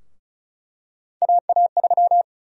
Morse code, heard as a keyed tone, about 36 wpm, 700 Hz. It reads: AA3
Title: USA Call Sign Prefixes - Repeated 2x - 35wpm